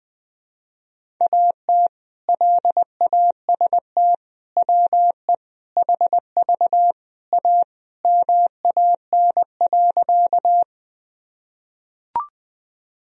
Morse code, heard as a keyed tone, about 21 wpm, 700 Hz. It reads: AT LAST WE HV A MAN.
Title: Sentences from Top 300 Words and Abbreviations 20wpm